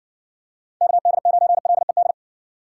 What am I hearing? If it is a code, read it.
Morse code, heard as a keyed tone, about 55 wpm, 700 Hz. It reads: KD9LED